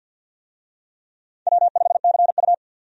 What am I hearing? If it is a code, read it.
Morse code, heard as a keyed tone, about 48 wpm, 700 Hz. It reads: W5CU
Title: USA Call Signs - 4 Characters - 50wpm